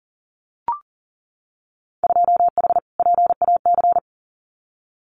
Morse code, heard as a keyed tone, about 41 wpm, 700 Hz. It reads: 2H PAC
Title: Winter Field Day Exchanges Encore 40wpm